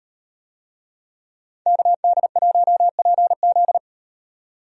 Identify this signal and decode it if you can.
Morse code, heard as a keyed tone, about 38 wpm, 700 Hz. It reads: KD1PZ